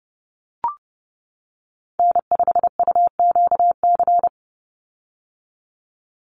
Morse code, heard as a keyed tone, about 30 wpm, 700 Hz. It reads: N5UQC
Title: USA Call Signs 30wpm